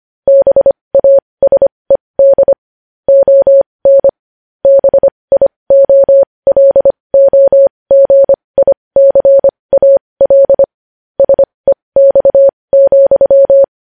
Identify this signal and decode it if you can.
Morse code, heard as a keyed tone, about 25 wpm, spaced as for 20 wpm, 550 Hz. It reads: BASED ON BIOLOGICAL SEX,
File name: podcast / News headlines-1x-25x20